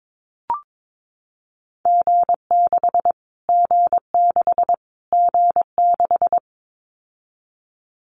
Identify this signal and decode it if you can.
Morse code, heard as a keyed tone, about 22 wpm, 700 Hz. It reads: G6 G6 G6
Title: Callsign Prefixes for Top 25 DXCC - Repeated 3x - 22wpm